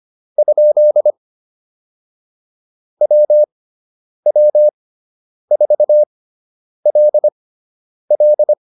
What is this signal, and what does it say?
Morse code, heard as a keyed tone, about 25 wpm, spaced as for 9 wpm, 600 Hz. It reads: ? WW4LL
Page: fundamental-practice-files